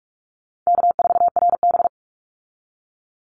Morse code, heard as a keyed tone, about 44 wpm, 700 Hz. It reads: K4RB
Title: Easy Call Signs Repeated 2x - 45wpm